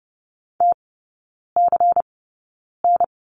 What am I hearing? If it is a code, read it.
Morse code, heard as a keyed tone, about 30 wpm, spaced as for 9 wpm, 700 Hz. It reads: TCN
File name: Lesson 18e - Letter m - Three Characters 30wpm